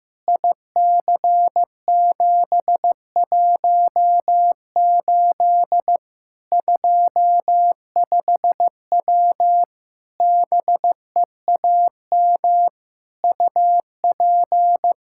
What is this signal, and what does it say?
Morse code, heard as a keyed tone, about 15 wpm, 700 Hz. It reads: IC718 25W BEAM UP
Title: K1USN SST QSO Practice 15wpm